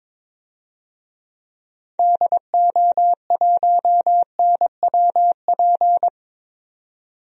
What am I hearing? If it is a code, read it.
Morse code, heard as a keyed tone, about 22 wpm, 700 Hz. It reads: DO1NWP
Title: DXCC-100 Call Signs - 6 Characters - 22wpm